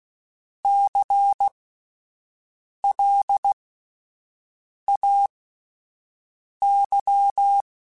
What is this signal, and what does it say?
Morse code, heard as a keyed tone, about 16 wpm, spaced as for 5 wpm, 800 Hz. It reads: CLAY
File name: Alphabetical / Alphabetical mp3s / Cs @5